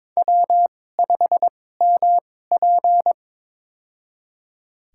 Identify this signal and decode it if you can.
Morse code, heard as a keyed tone, about 21 wpm, spaced as for 16 wpm, 700 Hz. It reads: W5MP
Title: USA Call Signs - 4 Characters - 2x Character Spacing - 22wpm